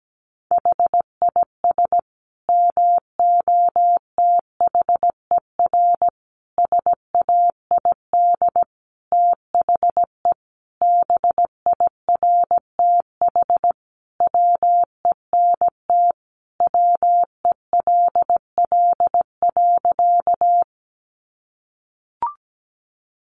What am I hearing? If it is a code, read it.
Morse code, heard as a keyed tone, about 17 wpm, 700 Hz. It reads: HIS MOTHER SAID THE BIRTH WENT WELL.
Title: Sentences from Top 800 Words 17wpm